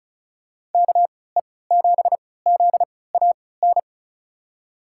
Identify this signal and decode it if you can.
Morse code, heard as a keyed tone, about 36 wpm, spaced as for 20 wpm, 700 Hz. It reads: KE7ZAN
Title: USA Call Signs - 6 Characters - 3x Character Spacing - 35wpm